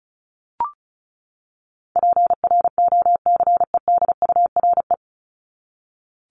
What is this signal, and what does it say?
Morse code, heard as a keyed tone, about 34 wpm, 700 Hz. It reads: PROCEDURE
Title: Top 1751-2000 Words 35wpm